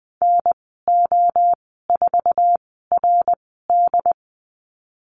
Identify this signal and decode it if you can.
Morse code, heard as a keyed tone, about 20 wpm, spaced as for 15 wpm, 700 Hz. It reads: NO4RD
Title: 076 Lesson 13g - Letter L - Call Signs 20wpm